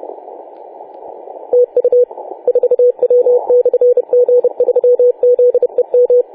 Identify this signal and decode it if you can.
Morse code, heard as a keyed tone, about 30 wpm, 490 Hz. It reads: TU 4W/G3ZEM
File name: 4W/G3ZEM 28.010MHz CW